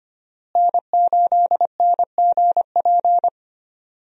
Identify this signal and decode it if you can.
Morse code, heard as a keyed tone, about 25 wpm, 700 Hz. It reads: N8NGP
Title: USA Call Signs - 5 Characters - Repeated 2x - 25wpm